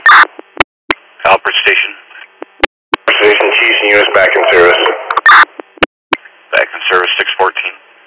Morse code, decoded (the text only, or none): none